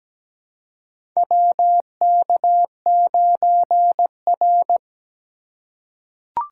Morse code, heard as a keyed tone, about 17 wpm, 700 Hz. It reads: WK9R
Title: USA Call Signs - 4 Characters - 3x Character Spacing - 17wpm